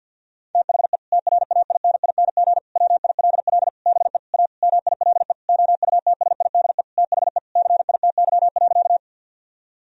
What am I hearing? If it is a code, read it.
Morse code, heard as a keyed tone, about 50 wpm, 700 Hz. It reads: THE TRAINING WILL BE A MILE OUTSIDE THE CITY.
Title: Sentences from Top 1000 Words - Review Part 4 - 50wpm